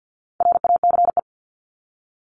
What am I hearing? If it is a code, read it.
Morse code, heard as a keyed tone, about 50 wpm, 700 Hz. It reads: RACE
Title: Top 1501-1750 Words 50wpm